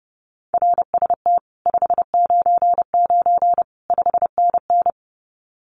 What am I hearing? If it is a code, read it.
Morse code, heard as a keyed tone, about 31 wpm, 700 Hz. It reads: RST 599 5NN
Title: Common QSO Phrases Encore 30wpm